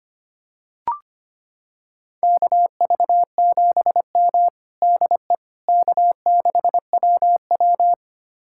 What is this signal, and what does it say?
Morse code, heard as a keyed tone, about 25 wpm, 700 Hz. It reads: KV7M DE K6WW
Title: USA Callsign DE Callsign - 4 Characters - 25wpm